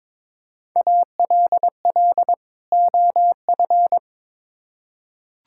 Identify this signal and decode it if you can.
Morse code, heard as a keyed tone, about 22 wpm, 700 Hz. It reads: ALL OF
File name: Build A Phrase - 2-Word Phrases - 22wpm